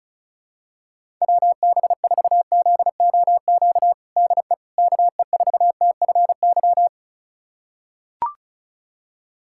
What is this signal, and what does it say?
Morse code, heard as a keyed tone, about 34 wpm, 700 Hz. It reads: WB4ZOQ DE KE4TFY
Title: USA Callsign DE Callsign 35wpm